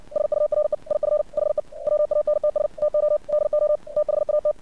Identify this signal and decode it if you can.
Morse code, heard as a keyed tone, about 60 wpm, 600 Hz. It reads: SURE IF HE WANTS TO DO THAT
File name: archived / CRSnet